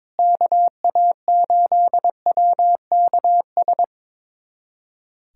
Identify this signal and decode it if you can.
Morse code, heard as a keyed tone, about 22 wpm, 700 Hz. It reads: KA8WKS
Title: USA Callsign DE Callsign 22wpm